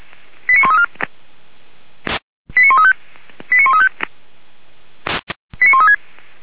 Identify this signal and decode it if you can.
Five-tone selective call (ZVEI), heard as quick runs of five tones, tones of about 70 ms. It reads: 89134 89135 89135 89136
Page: Server at audio.f85360.feuerwehr-orlamuende.org